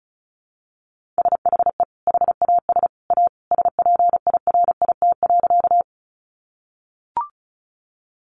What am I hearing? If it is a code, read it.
Morse code, heard as a keyed tone, about 34 wpm, 700 Hz. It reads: SHE HAS A SPIRIT.
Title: Sentences from Top 1400 Words 35wpm